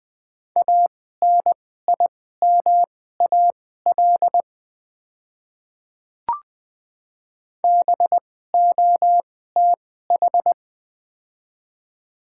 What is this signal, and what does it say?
Morse code, heard as a keyed tone, about 20 wpm, spaced as for 14 wpm, 700 Hz. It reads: ANIMAL BOTH
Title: 2x Character Spacing - Top 300 Words - 20wpm